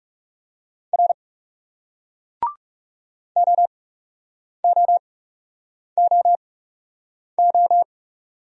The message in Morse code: R OOOO

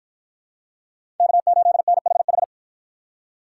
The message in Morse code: K8NHH

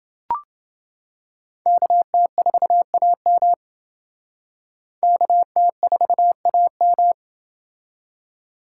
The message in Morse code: KT4AM KT4AM

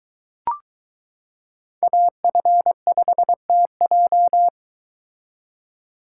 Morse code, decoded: AF5TJ

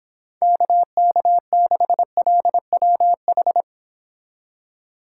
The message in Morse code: KK6LWH

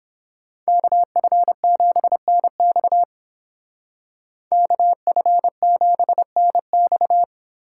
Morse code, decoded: KF7NX KF7NX

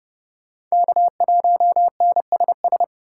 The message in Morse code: K1NSS

